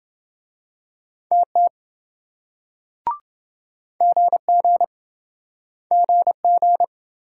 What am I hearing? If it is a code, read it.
Morse code, heard as a keyed tone, about 27 wpm, 700 Hz. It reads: TT GG GG